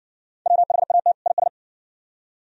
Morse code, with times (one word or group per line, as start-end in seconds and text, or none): 0.5-1.1 WHAT
1.3-1.5 IS